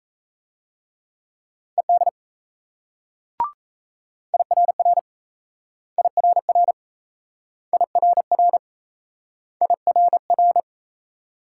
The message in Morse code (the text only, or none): ED IRR IRR IRR IRR